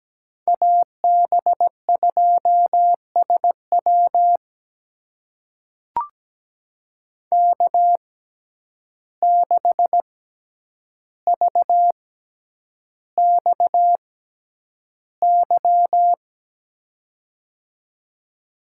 Morse code, heard as a keyed tone, about 17 wpm, spaced as for 9 wpm, 700 Hz. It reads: AB2SW K6VXY
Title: USA Call Signs - 6x Character Spacing - 17wpm